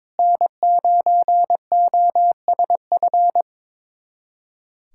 Morse code, heard as a keyed tone, about 22 wpm, 700 Hz. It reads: N9OSF
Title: USA Callsign DE Callsign - 5 Characters - 22wpm